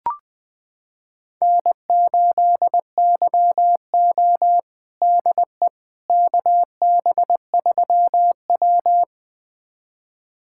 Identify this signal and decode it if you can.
Morse code, heard as a keyed tone, about 21 wpm, 700 Hz. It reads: N8YO DE KB3W
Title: USA Callsign DE Callsign - 4 Characters - 20wpm